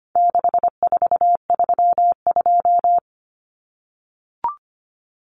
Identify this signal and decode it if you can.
Morse code, heard as a keyed tone, about 25 wpm, 700 Hz. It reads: 6432
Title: Common Cut Numbers - 4-Digit 25wpm